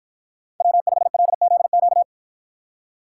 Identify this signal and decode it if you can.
Morse code, heard as a keyed tone, about 53 wpm, 700 Hz. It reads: W5LZQ